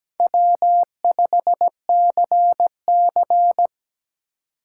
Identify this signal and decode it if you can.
Morse code, heard as a keyed tone, about 17 wpm, 700 Hz. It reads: W5CC
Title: USA Call Signs - 4 Characters - 2x Character Spacing - 17wpm